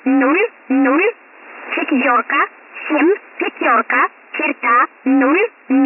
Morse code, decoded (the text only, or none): none